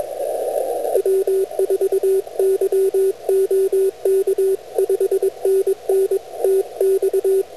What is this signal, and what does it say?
Morse code, heard as a keyed tone, about 21 wpm, 390 Hz, with radio noise. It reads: W4YOK 5NN TX